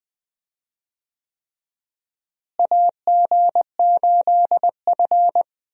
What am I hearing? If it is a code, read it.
Morse code, heard as a keyed tone, about 20 wpm, 700 Hz. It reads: AG8F
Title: USA Call Signs - 4 Characters - Repeated 3x - 20wpm